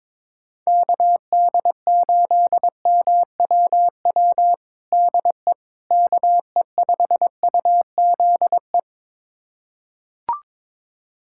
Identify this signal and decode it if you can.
Morse code, heard as a keyed tone, about 22 wpm, 700 Hz. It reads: KD8MWW DE KE5UZE
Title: USA Callsign DE Callsign - 6 Characters - 22wpm